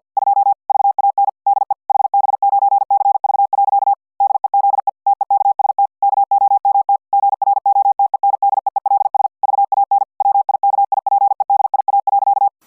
Morse code, heard as a keyed tone, about 50 wpm, 800 Hz. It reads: 2 VAN DE HB9CV. DEZE TEKST KOMT GROTENDEELS VAN WIKIPEDIA.